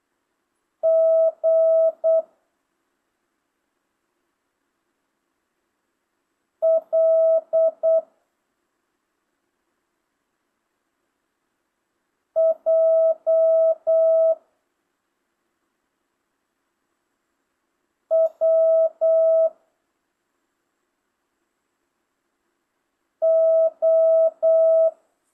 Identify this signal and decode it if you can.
Morse code, heard as a keyed tone, about 8 wpm, 650 Hz. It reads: G L J W O
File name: jh8cbh / 2025cw